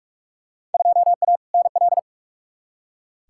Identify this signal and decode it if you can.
Morse code, heard as a keyed tone, about 45 wpm, 700 Hz. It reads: 2A NL
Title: Field Day Exchanges Encore - 45wpm